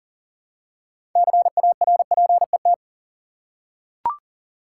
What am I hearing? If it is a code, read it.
Morse code, heard as a keyed tone, about 40 wpm, 700 Hz. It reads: CARPET
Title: Six Letter Words - 40wpm